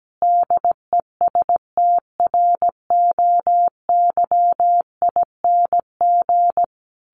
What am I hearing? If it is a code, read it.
Morse code, heard as a keyed tone, about 17 wpm, 700 Hz. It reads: DESTROYING
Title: A Sound Of Thunder 17wpm